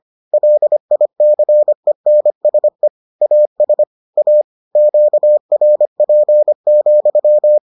Morse code, heard as a keyed tone, about 25 wpm, 600 Hz. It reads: LICENSE AS A QRP,